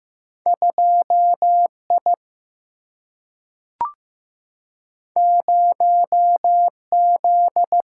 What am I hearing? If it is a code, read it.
Morse code, heard as a keyed tone, about 15 wpm, 700 Hz. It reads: 2I 0Z
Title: Single-Letter Single-Number 15wpm